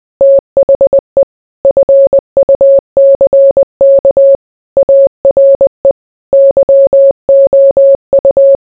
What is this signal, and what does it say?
Morse code, heard as a keyed tone, about 20 wpm, 550 Hz. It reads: THE FUCK ARE YOU